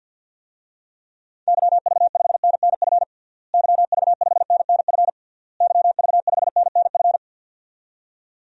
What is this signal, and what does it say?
Morse code, heard as a keyed tone, about 49 wpm, 700 Hz. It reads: YV5NNF YV5NNF YV5NNF
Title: DXCC-100 Call Signs - 6 Characters - Repeated 3x - 50wpm